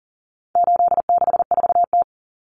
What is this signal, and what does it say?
Morse code, heard as a keyed tone, about 40 wpm, 700 Hz. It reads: 864T
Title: Common Cut Numbers - 4-Digit 40wpm